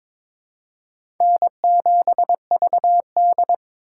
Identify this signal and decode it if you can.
Morse code, heard as a keyed tone, about 22 wpm, 700 Hz. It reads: N7VD
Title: USA Call Signs - 2x Character Spacing - 22wpm